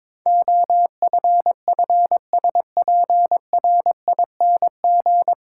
Morse code, heard as a keyed tone, about 22 wpm, 700 Hz. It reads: OFFSPRING
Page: Morse Code Ninja practice 22wpm